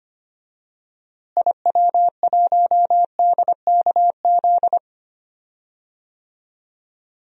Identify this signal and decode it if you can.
Morse code, heard as a keyed tone, about 25 wpm, 700 Hz. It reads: IW1DKZ
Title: DXCC-100 Call Signs - 6 Characters - Repeated 3x - 25wpm